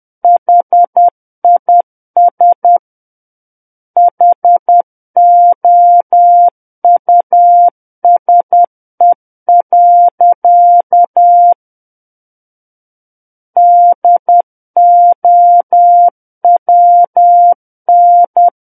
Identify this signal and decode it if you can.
Morse code, heard as a keyed tone, about 10 wpm, 700 Hz. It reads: HIS HOUSE. DOWN
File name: Book One / 01-Chapter 1 - 10 WPM